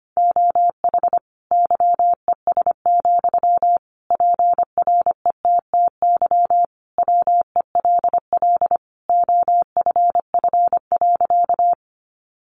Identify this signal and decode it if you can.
Morse code, heard as a keyed tone, about 25 wpm, 700 Hz. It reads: OH YES, PRETTY WELL OFF.
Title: Sentences from Top 1000 Words 25wpm